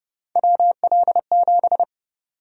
Morse code, heard as a keyed tone, about 32 wpm, 700 Hz. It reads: WL7
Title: USA Call Sign Prefixes - 30wpm